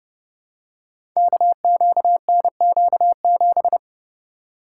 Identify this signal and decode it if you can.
Morse code, heard as a keyed tone, about 30 wpm, 700 Hz. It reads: KQNQ7
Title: Code Groups - 5 Letter:Numbers - 30wpm